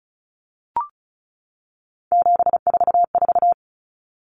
Morse code, heard as a keyed tone, about 34 wpm, 700 Hz. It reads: 744
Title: Numbers - 3-Digit 35wpm